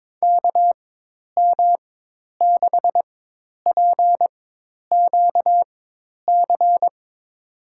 Morse code, heard as a keyed tone, about 22 wpm, spaced as for 10 wpm, 700 Hz. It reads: KM6PQC
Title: USA Call Signs - 4x Character Spacing - 22wpm